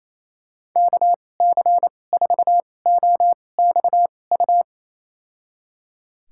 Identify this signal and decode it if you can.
Morse code, heard as a keyed tone, about 28 wpm, spaced as for 20 wpm, 700 Hz. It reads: KC4OXU
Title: USA Call Signs - 2x Character Spacing - 28wpm